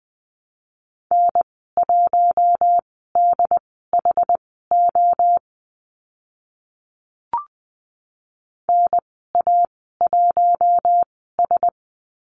Morse code, heard as a keyed tone, about 20 wpm, spaced as for 14 wpm, 700 Hz. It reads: N1DHO NA1S
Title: 069 Lesson 12g - Letter d - Call Signs 20wpm